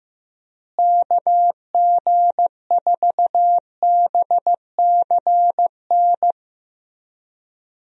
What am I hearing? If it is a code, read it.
Morse code, heard as a keyed tone, about 15 wpm, 700 Hz. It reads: KG4BCN
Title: USA Call Signs - 6 Characters - Repeated 3x - 15wpm